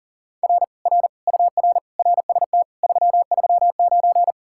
Morse code, heard as a keyed tone, about 52 wpm, 700 Hz. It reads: R R UR RST 339